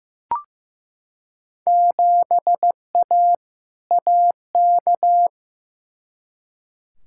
Morse code, heard as a keyed tone, about 15 wpm, 700 Hz. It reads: 7A AK
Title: Field Day Exchanges Encore - 15wpm